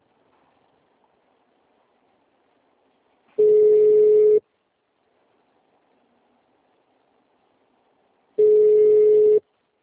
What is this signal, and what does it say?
Ringback tone (Europe, 425 Hz).